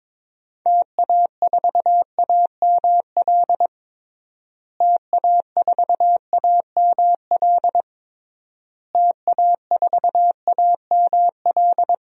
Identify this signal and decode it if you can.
Morse code, heard as a keyed tone, about 22 wpm, 700 Hz. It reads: TA4AML TA4AML TA4AML
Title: DXCC-100 Call Signs - 6 Characters - Repeated 3x - 22wpm